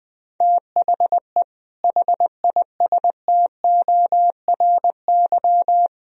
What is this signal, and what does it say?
Morse code, heard as a keyed tone, about 21 wpm, 700 Hz. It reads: THE HISTORY
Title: Speed-Racing - Sets of 4 Words 20wpm